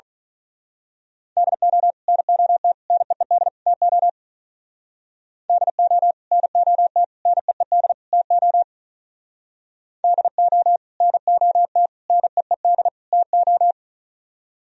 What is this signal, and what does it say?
Morse code, heard as a keyed tone, about 39 wpm, 700 Hz. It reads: DO NOT NEED TO DO NOT NEED TO DO NOT NEED TO
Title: Speed-Racing - Sets of 4 Words 35wpm